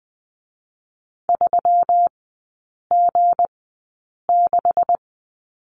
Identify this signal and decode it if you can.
Morse code, heard as a keyed tone, about 19 wpm, spaced as for 8 wpm, 700 Hz. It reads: 3G6
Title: 184 Lesson 30e - Letter b - Three Characters 20wpm